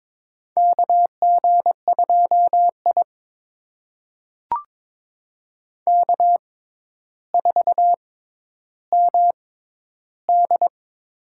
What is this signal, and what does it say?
Morse code, heard as a keyed tone, about 22 wpm, spaced as for 11 wpm, 700 Hz. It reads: KG2I K4MD
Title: USA Call Signs - 4 Characters - 6x Character Spacing - 22wpm